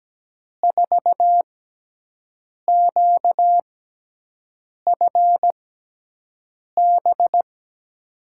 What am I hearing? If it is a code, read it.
Morse code, heard as a keyed tone, about 17 wpm, spaced as for 6 wpm, 700 Hz. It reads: 4QFB
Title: USA Call Signs - 6x Character Spacing - 17wpm